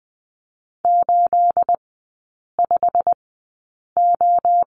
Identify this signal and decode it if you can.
Morse code, heard as a keyed tone, about 20 wpm, spaced as for 8 wpm, 700 Hz. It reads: 85O
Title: 212 Lesson 34e - Number 8 - Three Characters 20wpm